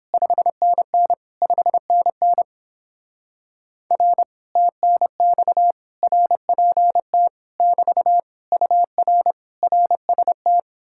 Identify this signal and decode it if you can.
Morse code, heard as a keyed tone, about 31 wpm, 700 Hz. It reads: 5NN 5NN R TNX RPT <BT> UR RST